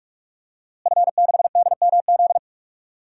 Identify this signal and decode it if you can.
Morse code, heard as a keyed tone, about 46 wpm, 700 Hz. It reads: F6DMZ
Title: DXCC-100 Call Signs - 5 Characters - Repeated 2x - 45wpm